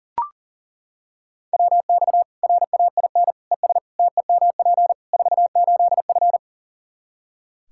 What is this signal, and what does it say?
Morse code, heard as a keyed tone, about 40 wpm, 700 Hz. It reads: WX RAIN ES TEMP 48F
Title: Practice QSO - Exchange 2 - Info - 40wpm